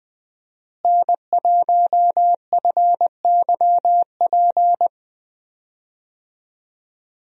Morse code, heard as a keyed tone, about 21 wpm, 700 Hz. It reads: N1FYP
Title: USA Call Signs - 5 Characters - Repeated 2x - 20wpm